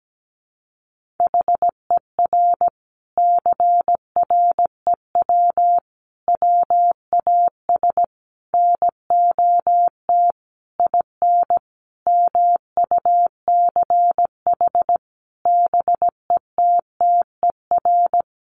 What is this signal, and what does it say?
Morse code, heard as a keyed tone, about 17 wpm, 700 Hz. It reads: HER CREW WAS NOT IN MUCH BETTER
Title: Sentences from Top 1500 Words 17wpm